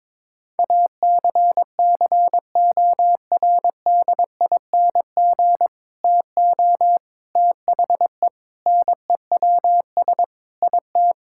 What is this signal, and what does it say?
Morse code, heard as a keyed tone, about 22 wpm, 700 Hz. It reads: ACCORDING TO THE NEWS IT
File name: practice / 22wpm / Sentences with 3-Word Phrases - 22wpm